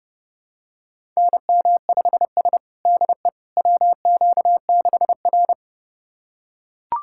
Morse code, heard as a keyed tone, about 32 wpm, 700 Hz. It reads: NM5S DE WQ6R
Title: USA Callsign DE Callsign - 4 Characters - 30wpm